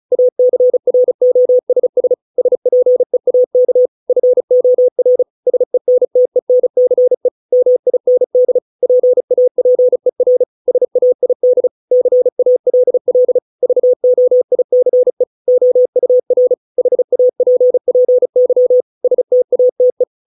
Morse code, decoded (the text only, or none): ACROSS SPEAK FOR SENTENCE MIND PAPER SAID CALL VOICE OUR HAPPY STATE